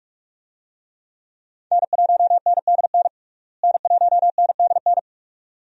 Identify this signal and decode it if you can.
Morse code, heard as a keyed tone, about 46 wpm, 700 Hz. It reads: N1NDN N1NDN
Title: USA Call Signs - 5 Characters - Repeated 2x - 45wpm